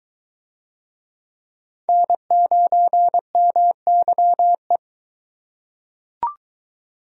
Morse code, heard as a keyed tone, about 21 wpm, 700 Hz. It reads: N9MYE